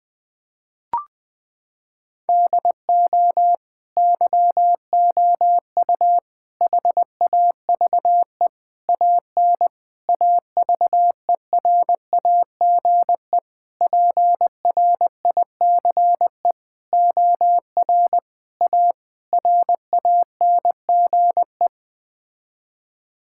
Morse code, heard as a keyed tone, about 20 wpm, 700 Hz. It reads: DO YOU HAVE AN AVERAGE PRICE OR A RANGE
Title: Sets of 9 Words - 20wpm